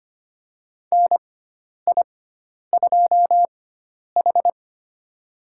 Morse code, heard as a keyed tone, about 25 wpm, spaced as for 10 wpm, 700 Hz. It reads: NI2H
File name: Lesson 16g - Letter c - Call Signs 25wpm